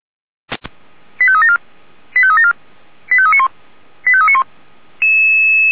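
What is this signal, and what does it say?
Five-tone selective call (ZVEI), heard as quick runs of five tones, tones of about 70 ms. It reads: 85374 85374 85391 85391
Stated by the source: ZVEI five-tone call received by SDR.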